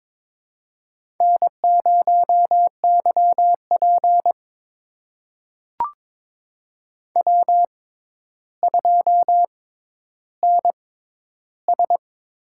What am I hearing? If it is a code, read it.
Morse code, heard as a keyed tone, about 22 wpm, spaced as for 11 wpm, 700 Hz. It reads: N0YP W2NS
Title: USA Call Signs - 4 Characters - 6x Character Spacing - 22wpm